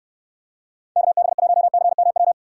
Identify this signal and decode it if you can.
Morse code, heard as a keyed tone, about 69 wpm, 700 Hz. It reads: KB1CMW